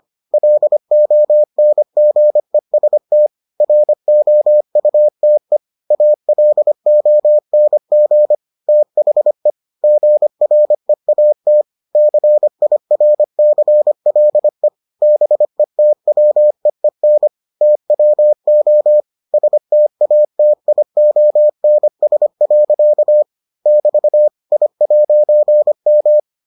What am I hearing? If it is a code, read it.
Morse code, heard as a keyed tone, about 25 wpm, 600 Hz. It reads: LONGEST ROUTE ALONG THE GREAT CIRCLE BETWEEN TWO STATIONS. <BT> I'M